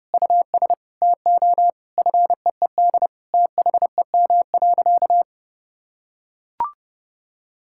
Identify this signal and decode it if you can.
Morse code, heard as a keyed tone, about 32 wpm, 700 Hz. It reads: US TO FEED THEM.
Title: Sentences from Top 1600 Words 30wpm